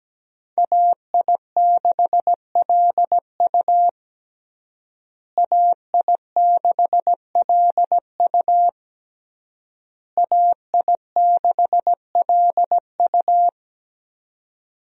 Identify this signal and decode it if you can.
Morse code, heard as a keyed tone, about 17 wpm, 700 Hz. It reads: AI6LU AI6LU AI6LU
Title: USA Call Signs - 5 Characters - Repeated 3x - 17wpm